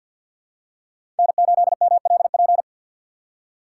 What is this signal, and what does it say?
Morse code, heard as a keyed tone, about 49 wpm, 700 Hz. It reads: N8MLP
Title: USA Call Signs - 5 Characters - 3x Character Spacing - 50wpm